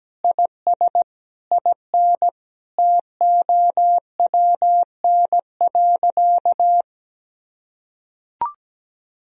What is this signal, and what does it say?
Morse code, heard as a keyed tone, about 17 wpm, 700 Hz. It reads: IS IN TOWN.
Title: Sentences from Top 1000 Words - Review Part 2 - 17wpm